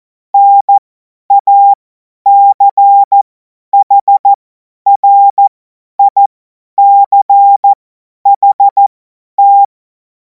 Morse code, heard as a keyed tone, about 14 wpm, spaced as for 10 wpm, 800 Hz. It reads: NACHRICHT
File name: Morsekurs / Witze / Witz 14. Juli 2026 50 F70